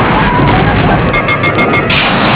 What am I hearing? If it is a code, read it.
There is no code.